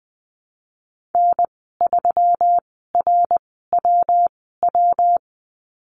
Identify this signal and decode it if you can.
Morse code, heard as a keyed tone, about 20 wpm, spaced as for 14 wpm, 700 Hz. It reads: N3RWW
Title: 161 Lesson 26g - Letter g - Call Signs 20wpm